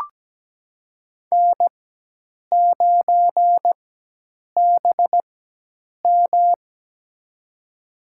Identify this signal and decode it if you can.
Morse code, heard as a keyed tone, about 17 wpm, spaced as for 8 wpm, 700 Hz. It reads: N9BM
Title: USA Call Signs - 4x Character Spacing - 17wpm